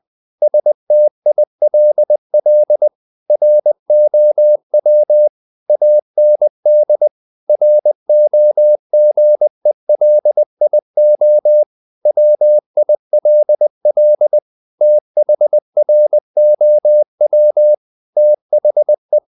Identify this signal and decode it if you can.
Morse code, heard as a keyed tone, about 20 wpm, 600 Hz. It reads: STILL ROW AND ROGELIO WILL THROW THE
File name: the old man and the sea / the old man and the sea part 1 20wpm